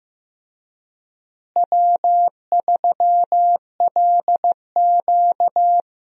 Morse code, heard as a keyed tone, about 15 wpm, 700 Hz. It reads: W3LQ
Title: USA Call Signs - 4 Characters - Repeated 2x - 15wpm